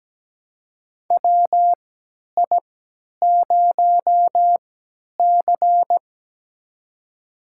Morse code, heard as a keyed tone, about 17 wpm, spaced as for 10 wpm, 700 Hz. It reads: WI0C
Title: USA Call Signs - 4 Characters - 3x Character Spacing - 17wpm